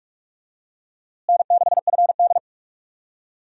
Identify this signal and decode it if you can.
Morse code, heard as a keyed tone, about 46 wpm, 700 Hz. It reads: N6FD
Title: USA Call Signs - 4 Characters - Repeated 2x - 45wpm